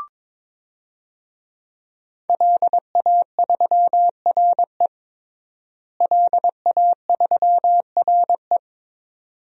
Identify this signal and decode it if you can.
Morse code, heard as a keyed tone, about 22 wpm, 700 Hz. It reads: LA3RE LA3RE
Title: DXCC-100 Call Signs - 5 Characters - Repeated 2x - 22wpm